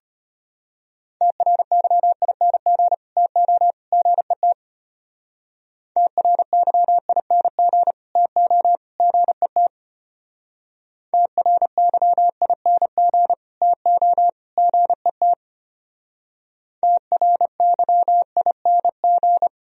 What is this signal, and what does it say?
Morse code, heard as a keyed tone, about 33 wpm, 700 Hz. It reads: TRYING TO GET TRYING TO GET TRYING TO GET TRYING